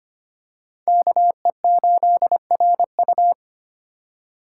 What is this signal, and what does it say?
Morse code, heard as a keyed tone, about 25 wpm, 700 Hz. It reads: KE8RU
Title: K1USN SST QSO Practice - 25wpm